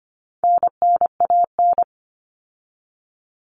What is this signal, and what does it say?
Morse code, heard as a keyed tone, about 25 wpm, 700 Hz. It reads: NNAN
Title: All Cut Numbers - 4-Digit 25wpm